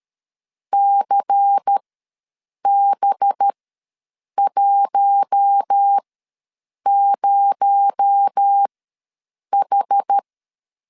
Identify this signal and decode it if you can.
Morse code, heard as a keyed tone, about 13 wpm, spaced as for 7 wpm, 800 Hz. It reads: CB10H
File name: LEZIONI / 8 YJ45 / 8 YJ45 ws 12wpm 3146